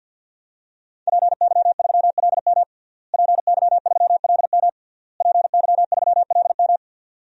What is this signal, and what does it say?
Morse code, heard as a keyed tone, about 48 wpm, 700 Hz. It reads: PY3LM PY3LM PY3LM
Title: DXCC-100 Call Signs - 5 Characters - Repeated 3x - 50wpm